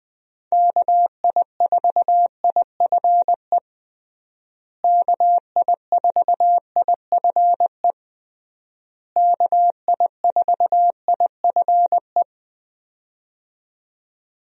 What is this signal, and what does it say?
Morse code, heard as a keyed tone, about 21 wpm, 700 Hz. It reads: KI4IFE KI4IFE KI4IFE
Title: USA Call Signs - 6 Characters - Repeated 3x - 20wpm